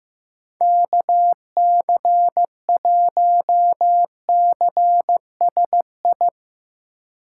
Morse code, heard as a keyed tone, about 15 wpm, 700 Hz. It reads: KC1CSI
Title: Speed Racing - USA 2x3 Call Signs 15wpm